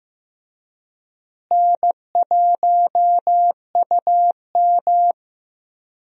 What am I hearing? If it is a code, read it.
Morse code, heard as a keyed tone, about 15 wpm, 700 Hz. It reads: N1UM
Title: USA Call Signs - 4 Characters - Repeated 2x - 15wpm